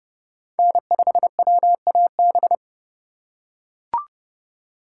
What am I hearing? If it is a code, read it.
Morse code, heard as a keyed tone, about 32 wpm, 700 Hz. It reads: N5WAB
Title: USA Call Signs - 3x Character Spacing - 30wpm